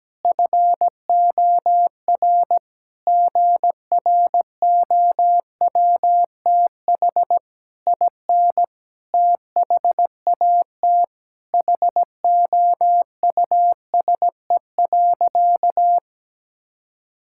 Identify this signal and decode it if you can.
Morse code, heard as a keyed tone, about 17 wpm, 700 Hz. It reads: FOR GROWTH IN THAT HOUSE.
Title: Sentences from Top 1000 Words - Review Part 5 - 17wpm